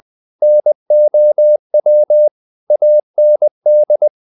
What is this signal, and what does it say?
Morse code, heard as a keyed tone, about 20 wpm, 600 Hz. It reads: NOW AND
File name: the old man and the sea / the old man and the sea part 1 20wpm